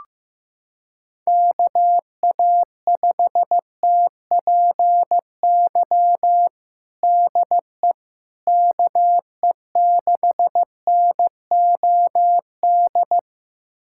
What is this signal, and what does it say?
Morse code, heard as a keyed tone, about 16 wpm, 700 Hz. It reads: KA5TPY DE KE6NOD
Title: USA Callsign DE Callsign - 6 Characters - 15wpm